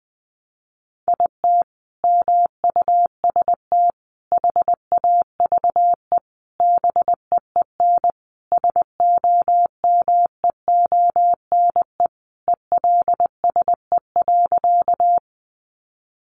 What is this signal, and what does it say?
Morse code, heard as a keyed tone, about 20 wpm, 700 Hz. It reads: IT MUST HAVE BEEN SOMEONE ELSE.
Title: Sentences from Top 800 Words 20wpm